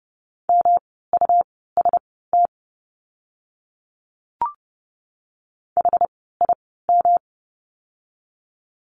Morse code, heard as a keyed tone, about 30 wpm, spaced as for 17 wpm, 700 Hz. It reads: MUST HIM
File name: course / 30:10wpm / 237 Lesson 38f - Letter q - Words 30wpm